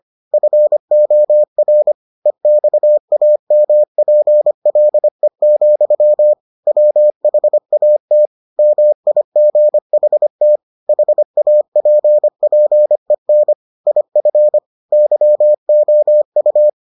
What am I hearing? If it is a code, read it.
Morse code, heard as a keyed tone, about 25 wpm, 600 Hz. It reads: FOR EXAMPLE, WHAT MIGHT HAPPEN IF YOU